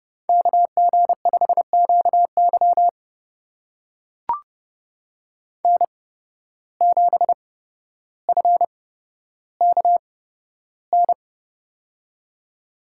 Morse code, heard as a keyed tone, about 30 wpm, spaced as for 13 wpm, 700 Hz. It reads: KG5QY N7FKN
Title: USA Call Signs - 5 Characters - 8x Character Spacing - 30wpm